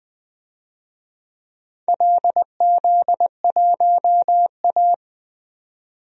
Morse code, heard as a keyed tone, about 20 wpm, 700 Hz. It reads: LZ1A
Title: DXCC-100 Call Signs - 4 Characters - 20wpm